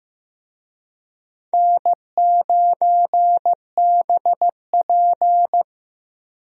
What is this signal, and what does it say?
Morse code, heard as a keyed tone, about 15 wpm, 700 Hz. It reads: N9BP
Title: USA Call Signs - 4 Characters - Repeated 2x - 15wpm